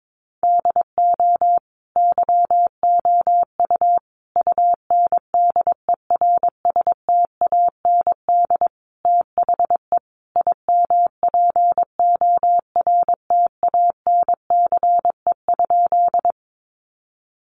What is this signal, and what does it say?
Morse code, heard as a keyed tone, about 22 wpm, 700 Hz. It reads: DO YOU UNDERSTAND THE IMPORTANCE?
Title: Sentences from Top 1200 Words 22wpm